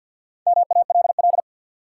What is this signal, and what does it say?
Morse code, heard as a keyed tone, about 49 wpm, 700 Hz. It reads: MALL
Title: Four Letter Words - 50wpm